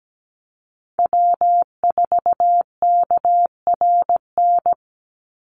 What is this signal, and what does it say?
Morse code, heard as a keyed tone, about 17 wpm, 700 Hz. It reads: W4KRN
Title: NA SOTA Chaser Callsigns 17wpm